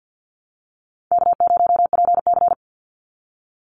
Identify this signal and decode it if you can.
Morse code, heard as a keyed tone, about 48 wpm, 700 Hz. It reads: K0PC
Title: Easy Call Signs - 50wpm